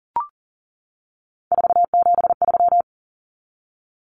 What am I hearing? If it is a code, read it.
Morse code, heard as a keyed tone, about 37 wpm, 700 Hz. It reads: 473
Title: Numbers - 3-Digit 40wpm